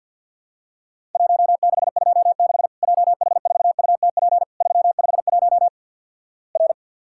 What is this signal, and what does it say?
Morse code, heard as a keyed tone, about 50 wpm, 700 Hz. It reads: 1626 JH4UTP 351 R
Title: Worked All Europe DX Contest - QTC Exchange with Repeats - 50wpm